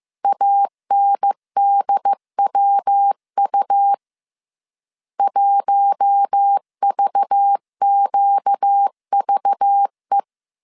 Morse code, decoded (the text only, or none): ANDWU 1VQVE